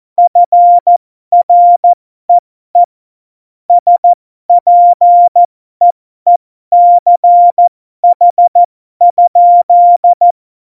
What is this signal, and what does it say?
Morse code, heard as a keyed tone, about 14 wpm, spaced as for 12 wpm, 700 Hz. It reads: FREE SPEECH?
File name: Morsekurs / Witze